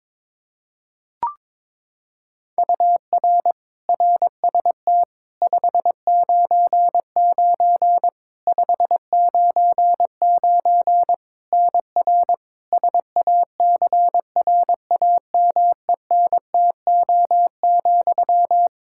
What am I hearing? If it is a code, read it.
Morse code, heard as a keyed tone, about 22 wpm, 700 Hz. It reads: UR RST 599 599 NR SACRAMENTO,
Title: K1USN SST QSO Practice 22wpm